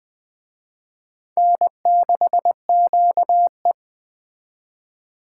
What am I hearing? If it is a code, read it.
Morse code, heard as a keyed tone, about 20 wpm, 700 Hz. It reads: N6QE